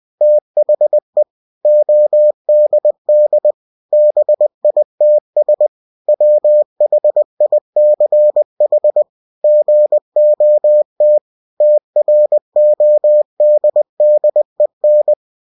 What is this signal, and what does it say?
Morse code, heard as a keyed tone, about 20 wpm, 600 Hz. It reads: THE ODD BITS WHICH GOT TRODDEN